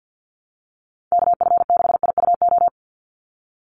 Morse code, heard as a keyed tone, about 50 wpm, 700 Hz. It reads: KF6IUO
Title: USA Call Signs 50wpm